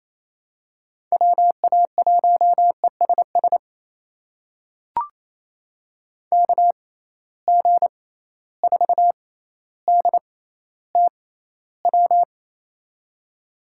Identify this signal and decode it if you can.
Morse code, heard as a keyed tone, about 28 wpm, spaced as for 14 wpm, 700 Hz. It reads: WA1ESS KG4DTW
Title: USA Call Signs - 6 Characters - 6x Character Spacing - 28wpm